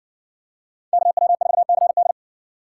Morse code, heard as a keyed tone, about 60 wpm, 700 Hz. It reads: KW4CD